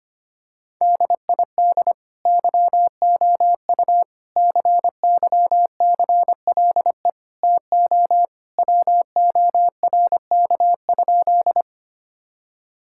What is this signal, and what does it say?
Morse code, heard as a keyed tone, about 25 wpm, 700 Hz. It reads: DID YOU CYCLE TO WORK?
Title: Sentences from Top 1000 Words - Review Part 6 - 25wpm